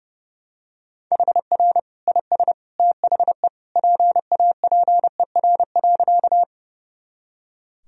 Morse code, heard as a keyed tone, about 32 wpm, 700 Hz. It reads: HR IS THE PAPER.
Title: Sentences from Top 300 Words and Abbreviations 30wpm